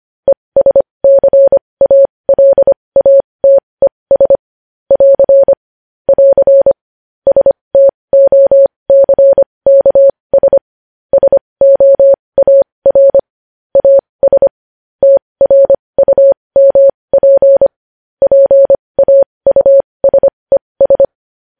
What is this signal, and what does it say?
Morse code, heard as a keyed tone, about 25 wpm, spaced as for 20 wpm, 550 Hz. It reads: ESCALATES <AR> <AR> STOCKS SOAR AS TRUMP PAUSES
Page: news headlines in morse code at 20 wpm